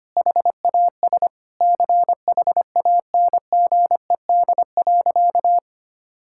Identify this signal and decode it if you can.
Morse code, heard as a keyed tone, about 25 wpm, 700 Hz. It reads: HAS CHANGED.
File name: practice / 25wpm / Sentences with 3-Word Phrases - 25wpm